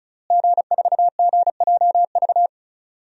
Morse code, heard as a keyed tone, about 36 wpm, 700 Hz. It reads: G4GJV
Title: DXCC-100 Call Signs - 5 Characters - Repeated 3x - 35wpm